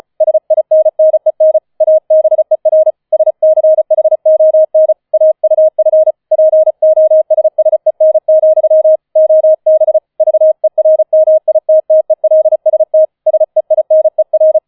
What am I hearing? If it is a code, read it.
Morse code, heard as a keyed tone, about 34 wpm, 630 Hz. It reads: SINNEN ABER SCHON AUF POSSEN, OB VERMITTELST SEINER
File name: 004 Vierter Streich 35wpm